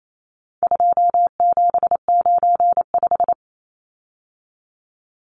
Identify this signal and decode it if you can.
Morse code, heard as a keyed tone, about 28 wpm, 700 Hz. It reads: 2795
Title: Numbers - 4-Digit 28wpm